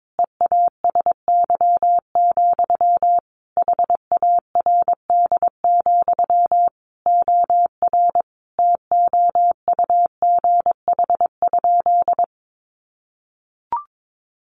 Morse code, heard as a keyed tone, about 22 wpm, 700 Hz. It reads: EASY, HARD, OR TOUGH?
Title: Sentences from Top 1200 Words 22wpm